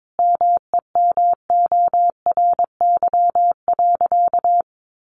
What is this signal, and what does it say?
Morse code, heard as a keyed tone, about 22 wpm, 700 Hz. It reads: MEMORY.
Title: Sentences from Top 1300 Words 22wpm